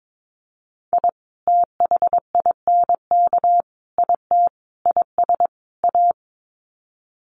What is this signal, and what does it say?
Morse code, heard as a keyed tone, about 22 wpm, 700 Hz. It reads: I THINK IT IS A
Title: Sets of 5 Words Encore 22wpm